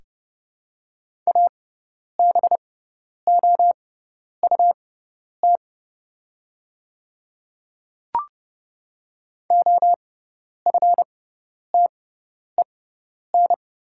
Morse code, heard as a keyed tone, about 30 wpm, spaced as for 10 wpm, 700 Hz. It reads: ABOUT OFTEN
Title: 6x Character Spacing - Top 300 Words - 30wpm